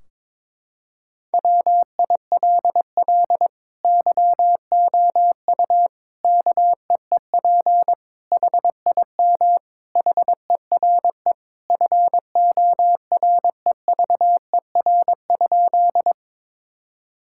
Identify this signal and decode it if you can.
Morse code, heard as a keyed tone, about 22 wpm, 700 Hz. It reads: WILL YOU KEEP HIM HERE FOREVER?
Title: Sentences from Top 1600 Words 22wpm